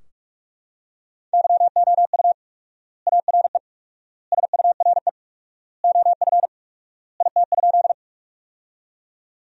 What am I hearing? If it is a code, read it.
Morse code, heard as a keyed tone, about 46 wpm, 700 Hz. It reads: YOU ARE SURE OF IT?
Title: 4x Word-Spacing - Sentences from Top 300 Words - 45wpm